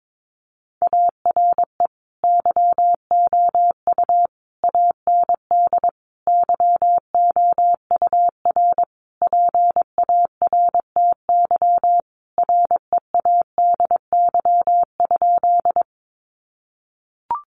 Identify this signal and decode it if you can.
Morse code, heard as a keyed tone, about 22 wpm, 700 Hz. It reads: ARE YOU AND YOUR PARTY READY?
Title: Sentences from Top 1400 Words 22wpm